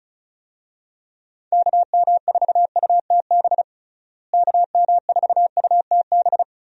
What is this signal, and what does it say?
Morse code, heard as a keyed tone, about 36 wpm, 700 Hz. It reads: KM4UTB KM4UTB
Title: USA Call Signs - 6 Characters - Repeated 2x - 35wpm